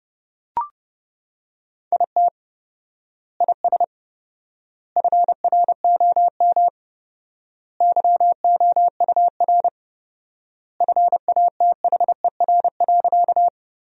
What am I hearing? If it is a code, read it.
Morse code, heard as a keyed tone, about 30 wpm, 700 Hz. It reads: IT IS FROM YOUR FATHER.
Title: 4x Word-Spacing - Sentences from Top 300 Words - 30wpm